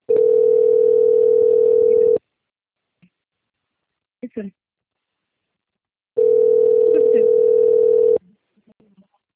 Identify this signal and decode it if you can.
Ringback tone (North America), 440 + 480 Hz, with speech.
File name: out- - - - - .31674